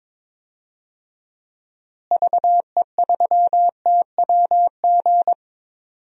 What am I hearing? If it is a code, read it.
Morse code, heard as a keyed tone, about 22 wpm, 700 Hz. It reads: VE3TWG
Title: DXCC-100 Call Signs - 6 Characters - Repeated 3x - 22wpm